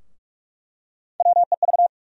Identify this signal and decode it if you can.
Morse code, heard as a keyed tone, about 42 wpm, 700 Hz. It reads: WEV
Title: USA Call Sign Postfixes - 3 Letters 45wpm